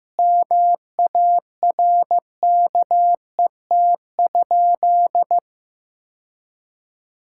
Morse code, heard as a keyed tone, about 15 wpm, 700 Hz. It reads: MARKET?
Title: Sentences from Top 1000 Words - Review Part 6 - 15wpm